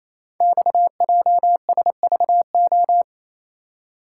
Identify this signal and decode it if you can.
Morse code, heard as a keyed tone, about 28 wpm, 700 Hz. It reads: XJSVO
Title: Code Groups - 5 Letters - 28wpm